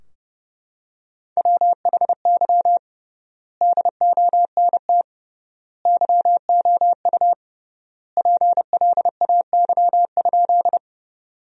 Morse code, heard as a keyed tone, about 30 wpm, 700 Hz. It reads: WHY DONT YOU PLAY?
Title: 3x Word-Spacing - Sentences from Top 300 Words - 30wpm